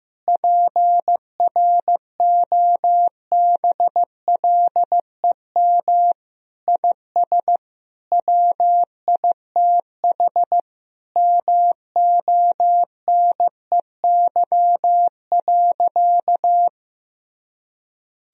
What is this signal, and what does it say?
Morse code, heard as a keyed tone, about 15 wpm, 700 Hz. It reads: PROBLEM IS WITH MONEY.
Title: Sentences from Top 1000 Words - Review Part 2 - 15wpm